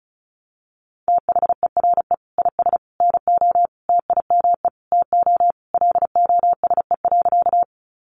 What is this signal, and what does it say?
Morse code, heard as a keyed tone, about 34 wpm, 700 Hz. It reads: THERE IS NO TIME TO LOSE.
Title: Sentences from Top 1100 Words 35wpm